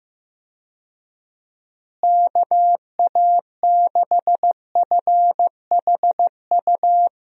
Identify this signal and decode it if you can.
Morse code, heard as a keyed tone, about 15 wpm, 700 Hz. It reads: KA6FHU
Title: USA Call Signs - 6 Characters - Repeated 3x - 15wpm